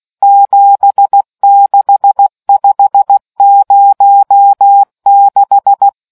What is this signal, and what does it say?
Morse code, heard as a keyed tone, about 16 wpm, 800 Hz. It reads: 76506